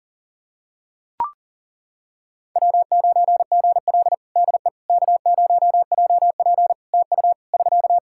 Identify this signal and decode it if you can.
Morse code, heard as a keyed tone, about 37 wpm, 700 Hz. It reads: W9GL DE K0JP TU <SK>
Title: Practice QSO - Exchange 4 - Ending - 40wpm